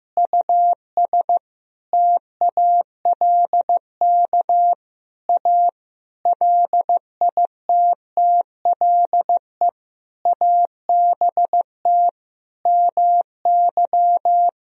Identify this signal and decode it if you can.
Morse code, heard as a keyed tone, about 15 wpm, 700 Hz. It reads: US TALK A LITTLE ABT MY
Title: Sentences from Top 400 Words and Abbreviations 15wpm